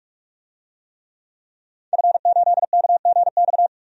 Morse code, heard as a keyed tone, about 46 wpm, 700 Hz. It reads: F8KGX
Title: DXCC-100 Call Signs - 5 Characters - 45wpm